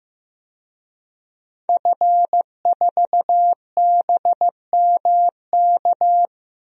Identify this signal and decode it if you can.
Morse code, heard as a keyed tone, about 15 wpm, 700 Hz. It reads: F4BMK
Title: DXCC-100 Call Signs - 5 Characters - 15wpm